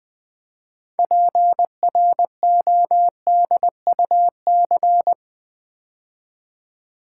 Morse code, heard as a keyed tone, about 20 wpm, 700 Hz. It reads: PRODUC
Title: Build A Word - 6 Letters At A Time - 12-Letter Words - 20wpm